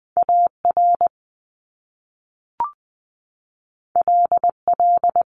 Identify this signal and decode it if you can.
Morse code, heard as a keyed tone, about 20 wpm, 700 Hz. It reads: A R L L
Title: Top Two Letter Combinations in Words 20wpm